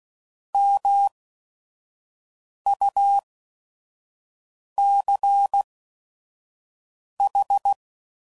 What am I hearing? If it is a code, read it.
Morse code, heard as a keyed tone, about 16 wpm, spaced as for 5 wpm, 800 Hz. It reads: MUCH